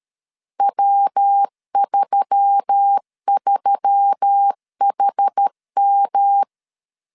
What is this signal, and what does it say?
Morse code, heard as a keyed tone, about 13 wpm, 800 Hz. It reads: W33HM
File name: CW / DET / 1 6 1500C S A 12wpm 3440